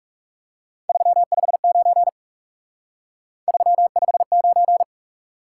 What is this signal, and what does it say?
Morse code, heard as a keyed tone, about 38 wpm, 700 Hz. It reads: 359 359
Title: Speed-Racing Top 180 QSO Elements 30wpm